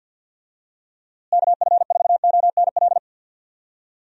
Morse code, heard as a keyed tone, about 47 wpm, 700 Hz. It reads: KF4ONL
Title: USA Call Signs - 6 Characters - Repeated 2x - 50wpm